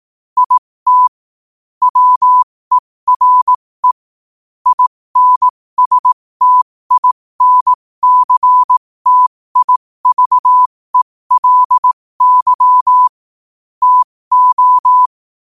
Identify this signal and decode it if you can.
Morse code, heard as a keyed tone, about 18 wpm, spaced as for 15 wpm, 1000 Hz. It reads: IT WERE INSTINCTIVELY TO
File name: qotdfiles / 15wpm